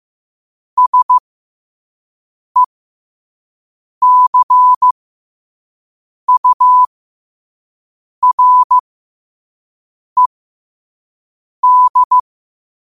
Morse code, heard as a keyed tone, about 15 wpm, spaced as for 5 wpm, 1000 Hz. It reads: SECURED
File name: qotdfiles / 5wpm